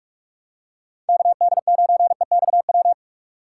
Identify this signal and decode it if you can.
Morse code, heard as a keyed tone, about 44 wpm, 700 Hz. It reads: KD9EXW
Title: USA Call Signs - 6x Character Spacing - 45wpm